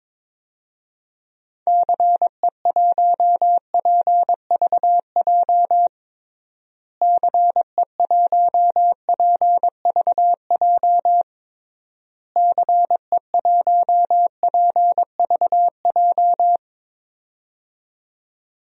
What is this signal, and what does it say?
Morse code, heard as a keyed tone, about 21 wpm, 700 Hz. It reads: CE1PVJ CE1PVJ CE1PVJ
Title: DXCC-100 Call Signs - 6 Characters - Repeated 3x - 22wpm